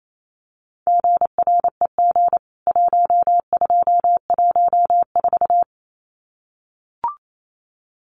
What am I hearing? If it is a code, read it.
Morse code, heard as a keyed tone, about 28 wpm, 700 Hz. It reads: GREG 1214
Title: CWT Exchange 28wpm